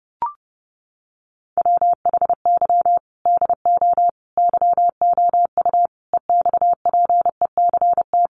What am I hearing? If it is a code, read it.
Morse code, heard as a keyed tone, about 30 wpm, 700 Hz. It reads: WHY DO YOU EXPECT
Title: Sentences from Top 1100 Words 30wpm